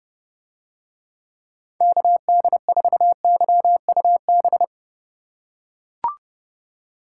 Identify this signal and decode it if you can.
Morse code, heard as a keyed tone, about 32 wpm, 700 Hz. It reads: KD4YUB